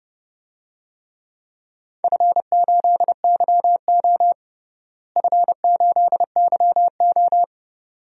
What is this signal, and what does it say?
Morse code, heard as a keyed tone, about 30 wpm, 700 Hz. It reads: F8YO F8YO
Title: DXCC-100 Call Signs - 4 Characters - Repeated 2x - 30wpm